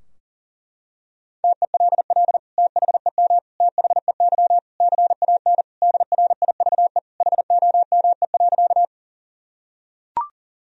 Morse code, heard as a keyed tone, about 40 wpm, 700 Hz. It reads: TELL THEM THEY CAN DRIVE HOME.
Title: Sentences from Top 1000 Words - Review Part 2 - 40wpm